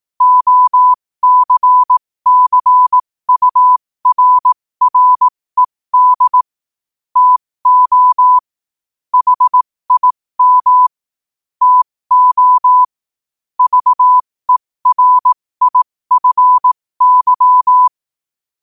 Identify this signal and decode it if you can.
Morse code, heard as a keyed tone, about 18 wpm, spaced as for 15 wpm, 1000 Hz. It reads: OCCURRED TO HIM TO VERIFY
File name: qotdfiles / 15wpm